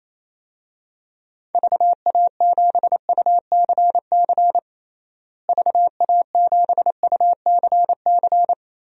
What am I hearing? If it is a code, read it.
Morse code, heard as a keyed tone, about 27 wpm, 700 Hz. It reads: VA7UCC VA7UCC
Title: DXCC-100 Call Signs - 6 Characters - Repeated 2x - 28wpm